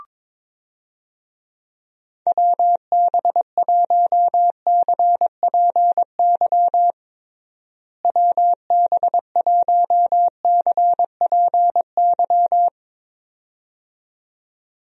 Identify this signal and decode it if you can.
Morse code, heard as a keyed tone, about 22 wpm, 700 Hz. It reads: WB1CPY WB1CPY
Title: USA Call Signs - 6 Characters - Repeated 2x - 22wpm